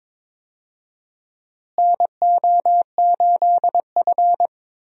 Morse code, heard as a keyed tone, about 22 wpm, 700 Hz. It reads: NO8F
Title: USA Call Signs - 4 Characters - Repeated 3x - 22wpm